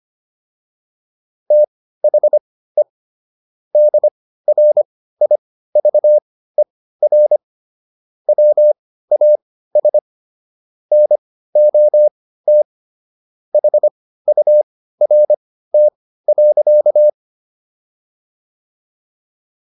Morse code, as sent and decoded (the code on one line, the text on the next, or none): - .... / -.. .-. .. ...- . .-. / .-- .- ... / -. --- - / .... ..- .-. - .-.-.-
THE DRIVER WAS NOT HURT.